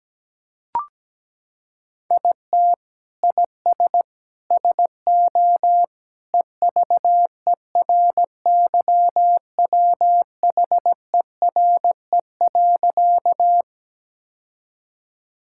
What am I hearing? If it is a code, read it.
Morse code, heard as a keyed tone, about 17 wpm, 700 Hz. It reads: IT IS SO EVERYWHERE.
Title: Sentences from Top 1600 Words 17wpm